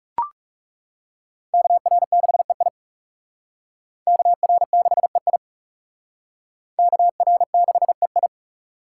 Morse code, heard as a keyed tone, about 35 wpm, 690 Hz. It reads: KR6EI KR6EI KR6EI
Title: Speed Racing - USA Call Signs 30wpm